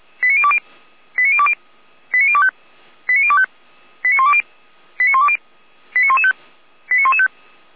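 Five-tone selective call (ZVEI), heard as quick runs of five tones, tones of about 70 ms. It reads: 89020 89020 89025 89025 89120 89120 89115 89115
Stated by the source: ZVEI five-tone selective call received by SDR.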